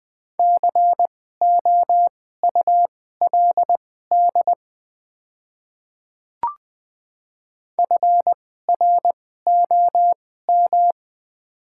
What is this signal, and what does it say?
Morse code, heard as a keyed tone, about 20 wpm, spaced as for 15 wpm, 700 Hz. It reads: COULD FROM
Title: 2x Character Spacing - Top 100 Words - 20wpm